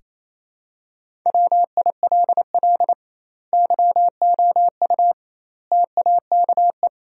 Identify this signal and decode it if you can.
Morse code, heard as a keyed tone, about 28 wpm, 700 Hz. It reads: WILL YOU TAKE
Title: 2x Word-Spacing - Sentences from Top 200 Words - 28wpm